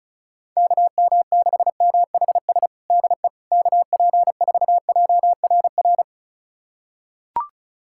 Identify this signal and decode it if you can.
Morse code, heard as a keyed tone, about 36 wpm, 700 Hz. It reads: KM6MHS DE KP4JRR
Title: USA Callsign DE Callsign - 6 Characters - 35wpm